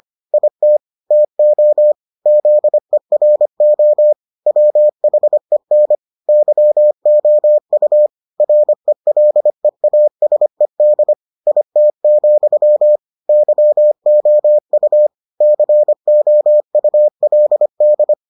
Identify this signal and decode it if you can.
Morse code, heard as a keyed tone, about 25 wpm, 600 Hz. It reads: IT TO ZERO WHEN YOU RELEASED IT, YOU COULD